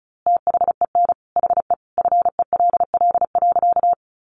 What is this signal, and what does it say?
Morse code, heard as a keyed tone, about 34 wpm, 700 Hz. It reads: THEN HE FELL.
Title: A Sound Of Thunder 35wpm